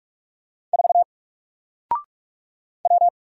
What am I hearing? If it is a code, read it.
Morse code, heard as a keyed tone, about 44 wpm, 700 Hz. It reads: 4 W
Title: Single Letter-Number 45wpm